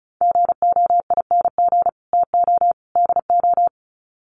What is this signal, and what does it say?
Morse code, heard as a keyed tone, about 34 wpm, 700 Hz. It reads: GOING TO DO
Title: Sets of 3 Words 35wpm